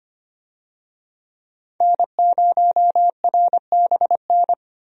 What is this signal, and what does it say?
Morse code, heard as a keyed tone, about 25 wpm, 700 Hz. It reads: N0RBN
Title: USA Call Signs - 5 Characters - 25wpm